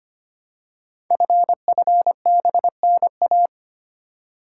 Morse code, heard as a keyed tone, about 25 wpm, 700 Hz. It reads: FFBNA
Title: Code Groups - 5 Letters - 25wpm